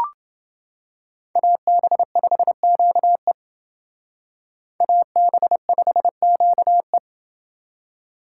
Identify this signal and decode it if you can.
Morse code, heard as a keyed tone, about 30 wpm, 700 Hz. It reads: AB5QE AB5QE